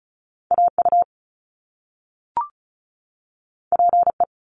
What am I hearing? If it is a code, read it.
Morse code, heard as a keyed tone, about 35 wpm, 700 Hz. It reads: A U P E
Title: Top Two Letter Combinations in Words 35wpm